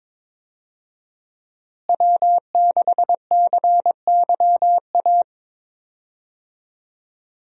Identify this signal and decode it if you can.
Morse code, heard as a keyed tone, about 22 wpm, 700 Hz. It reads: W6CYA
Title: USA Call Signs - 5 Characters - Repeated 2x - 22wpm